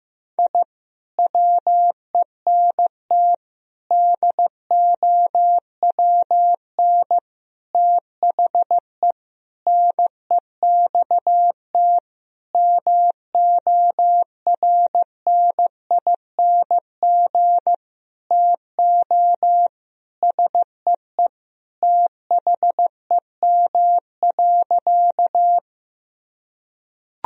Morse code, heard as a keyed tone, about 15 wpm, 700 Hz. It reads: I WENT DOWN THE NEXT MORNING TO SEE THEM.
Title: Sentences from Top 1000 Words - Review Part 1 - 15wpm